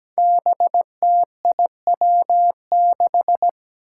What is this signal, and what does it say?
Morse code, heard as a keyed tone, about 17 wpm, 700 Hz. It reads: BTIW6
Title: Five Challenging Letter-Number Combinations 17wpm